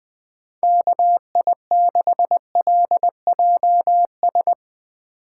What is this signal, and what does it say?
Morse code, heard as a keyed tone, about 22 wpm, 700 Hz. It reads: KI6LJS
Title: Speed Racing - USA 2x3 Call Signs 20wpm